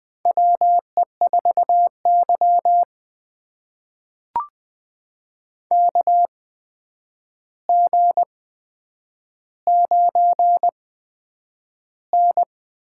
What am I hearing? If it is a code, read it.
Morse code, heard as a keyed tone, about 20 wpm, spaced as for 9 wpm, 700 Hz. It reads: WE4Y KG9N
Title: USA Call Signs - 4 Characters - 8x Character Spacing - 20wpm